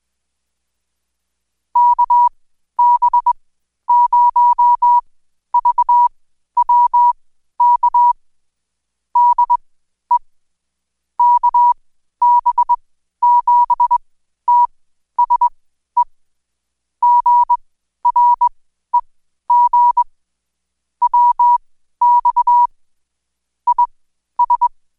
Morse code, heard as a keyed tone, about 21 wpm, spaced as for 12 wpm, 970 Hz. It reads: KB0VWK DE KB7TSE GREG WX IS
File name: Code Quick / AudioDownld2 / Seg121-Day26-DayDrill 26